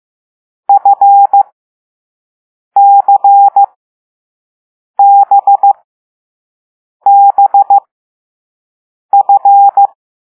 Morse code, heard as a keyed tone, about 15 wpm, spaced as for 5 wpm, 800 Hz. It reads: FCBBF
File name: Morse / Level6B